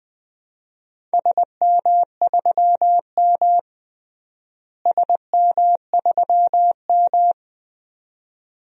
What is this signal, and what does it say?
Morse code, heard as a keyed tone, about 20 wpm, 700 Hz. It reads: SM3M SM3M
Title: DXCC-100 Call Signs - 4 Characters - Repeated 2x - 20wpm